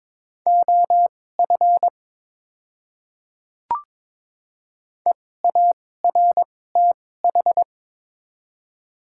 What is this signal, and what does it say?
Morse code, heard as a keyed tone, about 21 wpm, spaced as for 16 wpm, 700 Hz. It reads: OF EARTH
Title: 2x Character Spacing - Top 500 Words - 22wpm